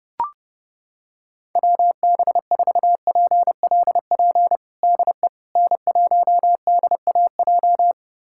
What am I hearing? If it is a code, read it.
Morse code, heard as a keyed tone, about 32 wpm, 700 Hz. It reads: WB4PLP DE N1DAJ
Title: USA Callsign DE Callsign 30wpm